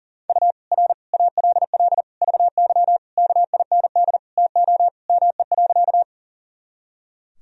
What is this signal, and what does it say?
Morse code, heard as a keyed tone, about 40 wpm, 700 Hz. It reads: U R ALL VY KIND TO ME.
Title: Sentences from Top 200 Words and Abbreviations 40wpm